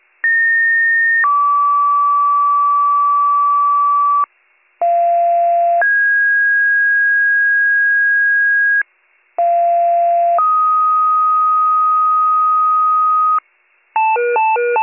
Two-tone page, heard as a one-second tone, then a three-second tone, about 1850 then 1150 Hz, about 690 then 1650 Hz, about 690 then 1250 Hz, with noise. It is heard from 0.2 to 13.4 s.